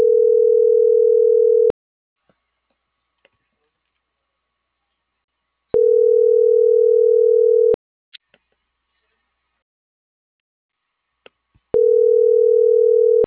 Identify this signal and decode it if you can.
Ringback tone (North America), 440 + 480 Hz.